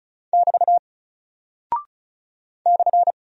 Morse code, heard as a keyed tone, about 36 wpm, 700 Hz. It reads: <BT> /
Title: Single Character (Letter:Number:Punctuation:Prosign) Alt 35wpm